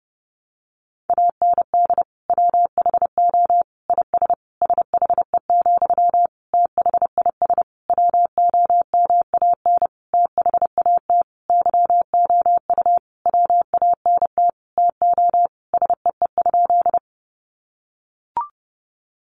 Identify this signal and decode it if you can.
Morse code, heard as a keyed tone, about 30 wpm, 700 Hz. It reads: AND WHO IS SHE, THIS WOMAN THAT YOU WANT TO SEE?
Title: Sentences from Top 1100 Words 30wpm